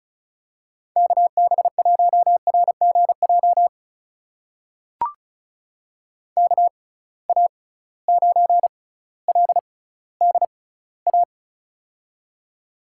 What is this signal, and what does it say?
Morse code, heard as a keyed tone, about 36 wpm, spaced as for 18 wpm, 700 Hz. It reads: KB1RGJ KA9LDA
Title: USA Call Signs - 6 Characters - 6x Character Spacing - 35wpm